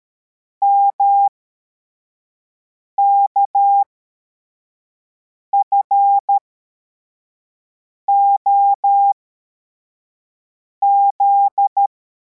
Morse code, heard as a keyed tone, about 13 wpm, spaced as for 4 wpm, 800 Hz. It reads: MKFOZ
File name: formation / cw / Text61